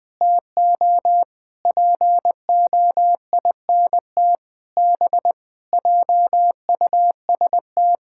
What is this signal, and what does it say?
Morse code, heard as a keyed tone, about 20 wpm, 700 Hz. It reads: TO POINT B JUST
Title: Sentences with 8-Word Phrases - 20wpm